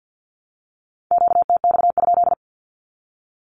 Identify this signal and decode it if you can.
Morse code, heard as a keyed tone, about 48 wpm, 700 Hz. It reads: QTX?
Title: Q-Codes Repeated 3x 50wpm